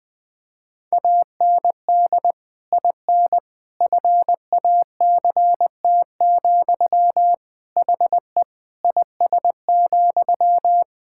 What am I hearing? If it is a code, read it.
Morse code, heard as a keyed tone, about 20 wpm, 700 Hz. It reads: AND IN FACT, HE IS,
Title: Sentences from Top 1600 Words 20wpm